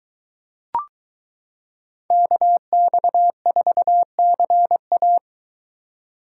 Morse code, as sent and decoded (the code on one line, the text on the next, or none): -.- -..- ....- -.-. .-
KX4CA